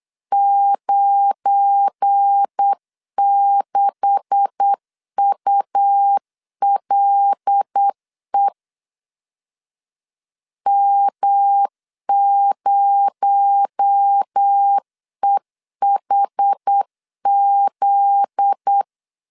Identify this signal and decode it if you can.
Morse code, heard as a keyed tone, about 8 wpm, 800 Hz. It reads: 96ULE M0EHZ
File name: LEZIONI / 9 286 8wpm 3425